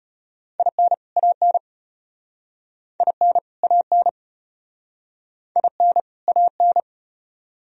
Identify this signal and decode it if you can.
Morse code, heard as a keyed tone, about 33 wpm, 700 Hz. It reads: IN AN IN AN IN AN